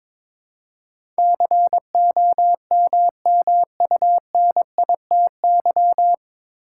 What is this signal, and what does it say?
Morse code, heard as a keyed tone, about 22 wpm, 700 Hz. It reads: COMMUNITY
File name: 9 Letter Words - First 3 Letters Given - 22wpm